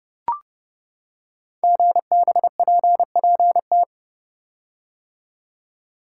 Morse code, heard as a keyed tone, about 30 wpm, 700 Hz. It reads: GBPPT
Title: Code Groups - 5 Letters - 30wpm